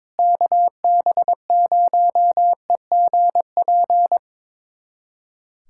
Morse code, heard as a keyed tone, about 21 wpm, 700 Hz. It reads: KB0EGP
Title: USA Callsign DE Callsign 22wpm